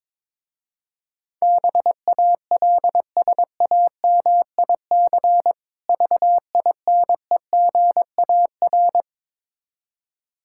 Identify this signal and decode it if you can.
Morse code, heard as a keyed tone, about 22 wpm, 700 Hz. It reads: BALSAMIC VINEGAR
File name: Difficult Words with Hints - 22wpm